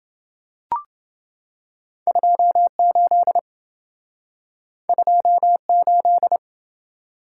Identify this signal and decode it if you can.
Morse code, heard as a keyed tone, about 28 wpm, 700 Hz. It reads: 28 28